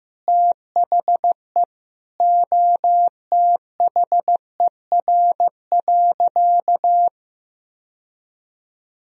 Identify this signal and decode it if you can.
Morse code, heard as a keyed tone, about 16 wpm, 700 Hz. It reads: THE OTHER.
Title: Sentences from Top 1000 Words - Review Part 2 - 15wpm